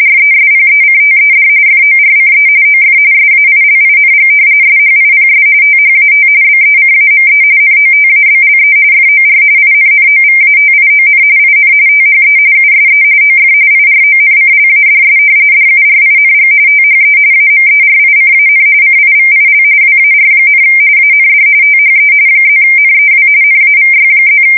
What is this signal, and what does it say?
RTTY, about 45.45 baud. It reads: RESPECTED FORCE OF SAILING SHIPS THAT BECAME NOTABLE FOR INNOVATION IN THE USE OF IRONCLADS DURING THE AMERICAN CIVIL WAR, AND THE +NEW NAVY+
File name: fsk170 / usnavy1